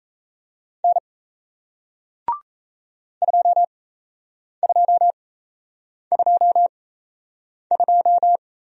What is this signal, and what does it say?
Morse code, heard as a keyed tone, about 38 wpm, spaced as for 8 wpm, 700 Hz. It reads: N 2222